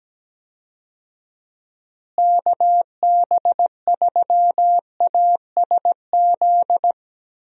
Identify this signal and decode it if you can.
Morse code, heard as a keyed tone, about 17 wpm, 700 Hz. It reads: KB3ASZ
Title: USA Call Signs - 6 Characters - Repeated 2x - 17wpm